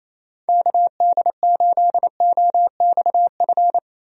Morse code, heard as a keyed tone, about 28 wpm, 700 Hz. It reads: KD8OXF
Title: USA Call Signs - 6 Characters - Repeated 3x - 28wpm